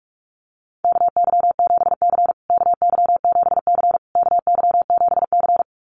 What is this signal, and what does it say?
Morse code, heard as a keyed tone, about 44 wpm, 700 Hz. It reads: KY7C KY7C KY7C
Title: USA Call Signs Repeated 3x 45wpm